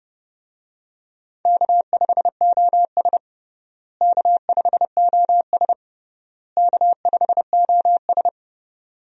Morse code, heard as a keyed tone, about 30 wpm, 700 Hz. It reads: K5OS K5OS K5OS
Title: USA Call Signs - 4 Characters - Repeated 3x - 30wpm